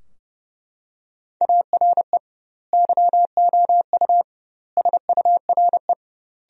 Morse code, heard as a keyed tone, about 30 wpm, 700 Hz. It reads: ARE YOU SURE
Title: 2x Word-Spacing - Sentences from Top 300 Words - 30wpm